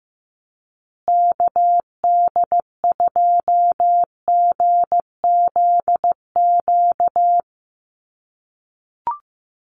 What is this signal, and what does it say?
Morse code, heard as a keyed tone, about 15 wpm, 700 Hz. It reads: KD2GZQ
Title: USA Call Signs Repeated 3x 15wpm